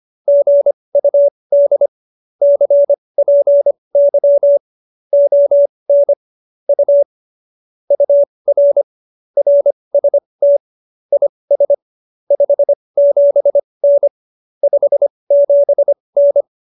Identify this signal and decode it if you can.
Morse code, heard as a keyed tone, about 25 wpm, spaced as for 20 wpm, 570 Hz. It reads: GUD CPY ON U UR RST IS 57N 57N
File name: qso215b 20